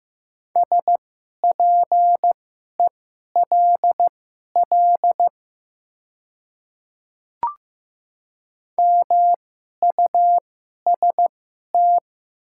Morse code, heard as a keyed tone, about 15 wpm, spaced as for 11 wpm, 700 Hz. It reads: SPELL MUST
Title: 2x Character Spacing - Top 200 Words - 15wpm